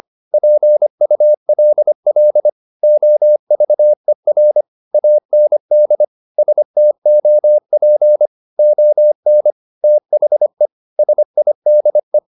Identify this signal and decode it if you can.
Morse code, heard as a keyed tone, about 25 wpm, 600 Hz. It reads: PULL OVER AND STOP ON THE SIDE